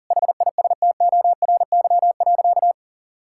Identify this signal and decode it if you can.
Morse code, heard as a keyed tone, about 40 wpm, 700 Hz. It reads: HISTORY.
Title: Sentences from Top 1000 Words - Review Part 5 - 40wpm